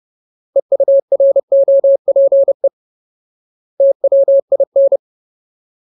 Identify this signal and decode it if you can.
Morse code, heard as a keyed tone, about 30 wpm, 550 Hz. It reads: EUROPE TWIN
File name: Ultimate Word List / 30WPM Ultimate